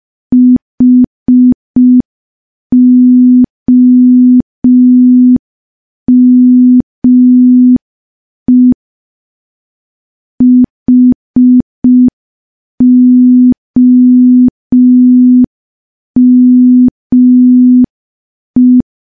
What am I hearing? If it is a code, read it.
Morse code, heard as a keyed tone, about 5 wpm, 250 Hz. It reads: HOME HOME
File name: morse / morse-training-05WPM